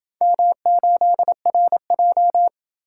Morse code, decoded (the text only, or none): M8RJ